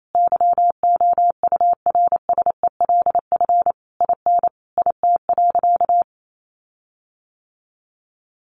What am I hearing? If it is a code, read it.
Morse code, heard as a keyed tone, about 28 wpm, 700 Hz. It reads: YOURSELF IN IT.
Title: Sentences from Top 1300 Words 28wpm